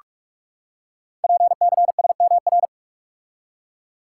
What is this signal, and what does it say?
Morse code, heard as a keyed tone, about 44 wpm, 700 Hz. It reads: PCSMR
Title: Code Groups - 5 Letter:Numbers - 45wpm